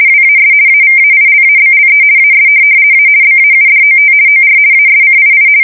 RTTY, about 50 baud. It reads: WAS AVAILABLE AS A BAUDOT OR ASCII
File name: fsk170-50B / teletype-corp